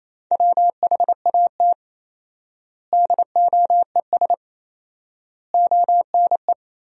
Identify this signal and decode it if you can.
Morse code, heard as a keyed tone, about 28 wpm, 700 Hz. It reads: WHAT DOES ONE
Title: 4x Word-Spacing - Sentences from Top 200 Words - 28wpm